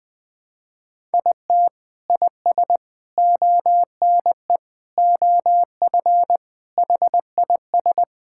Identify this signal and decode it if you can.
Morse code, heard as a keyed tone, about 21 wpm, 700 Hz. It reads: IT IS ONE OF HIS
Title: Sentences with 3-Word Phrases - 20wpm